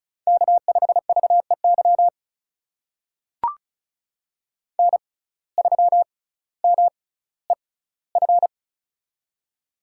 Morse code, heard as a keyed tone, about 36 wpm, spaced as for 18 wpm, 700 Hz. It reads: K5VEY N3MEF
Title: USA Call Signs - 5 Characters - 6x Character Spacing - 35wpm